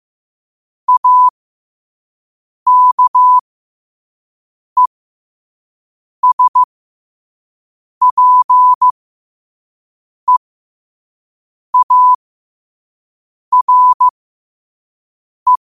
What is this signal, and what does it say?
Morse code, heard as a keyed tone, about 15 wpm, spaced as for 5 wpm, 1000 Hz. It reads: AKESPEARE
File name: qotdfiles / 5wpm